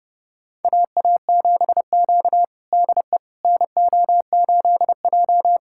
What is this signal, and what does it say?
Morse code, heard as a keyed tone, about 30 wpm, 700 Hz. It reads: AA7Q DE NO8J
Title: USA Callsign DE Callsign - 4 Characters - 30wpm